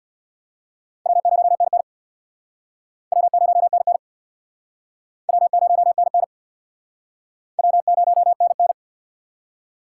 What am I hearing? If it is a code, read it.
Morse code, heard as a keyed tone, about 65 wpm, 700 Hz. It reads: W0NN W0NN W0NN W0NN